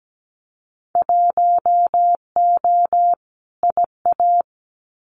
Morse code, heard as a keyed tone, about 17 wpm, 700 Hz. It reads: 1O IA
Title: Winter Field Day Exchanges 17wpm